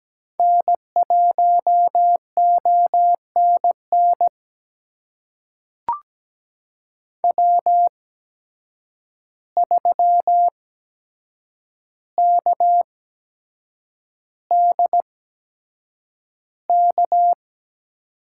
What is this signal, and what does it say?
Morse code, heard as a keyed tone, about 17 wpm, spaced as for 7 wpm, 700 Hz. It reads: N1ONN W3KDK
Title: USA Call Signs - 5 Characters - 8x Character Spacing - 17wpm